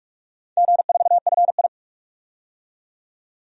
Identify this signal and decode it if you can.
Morse code, heard as a keyed tone, about 41 wpm, 700 Hz. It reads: G4FI
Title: DXCC-100 Call Signs - 4 Characters - 45wpm